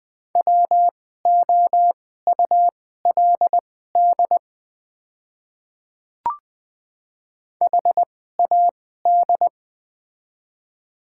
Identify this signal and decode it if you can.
Morse code, heard as a keyed tone, about 20 wpm, spaced as for 14 wpm, 700 Hz. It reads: WOULD HAD
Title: 2x Character Spacing - Top 100 Words - 20wpm